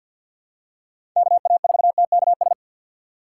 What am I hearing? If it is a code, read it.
Morse code, heard as a keyed tone, about 46 wpm, 700 Hz. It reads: KA4TKS
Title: USA Call Signs - 6 Characters - 3x Character Spacing - 50wpm